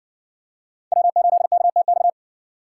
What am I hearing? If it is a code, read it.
Morse code, heard as a keyed tone, about 60 wpm, 700 Hz. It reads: W8KTX